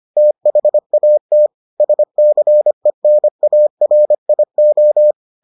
Morse code, heard as a keyed tone, about 25 wpm, 600 Hz. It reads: THAT SCENARIO